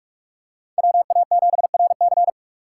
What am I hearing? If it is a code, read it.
Morse code, heard as a keyed tone, about 46 wpm, 700 Hz. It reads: WA7RC
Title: USA Call Signs - 5 Characters - 3x Character Spacing - 45wpm